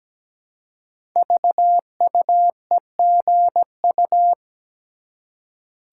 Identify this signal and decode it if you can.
Morse code, heard as a keyed tone, about 17 wpm, 700 Hz. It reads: VUEGU
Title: Code Groups - 5 Letters - 17wpm